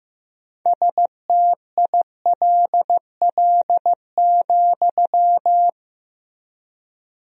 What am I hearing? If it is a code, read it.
Morse code, heard as a keyed tone, about 15 wpm, 700 Hz. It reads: STILL,
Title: 3x Word-Spacing - Sentences from Top 300 Words - 15wpm